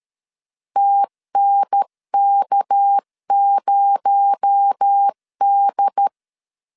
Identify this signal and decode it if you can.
Morse code, heard as a keyed tone, about 13 wpm, 800 Hz. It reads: TNK0D
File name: LEZIONI / 7 GF9Z / 7 GF9Z 12wpm 2147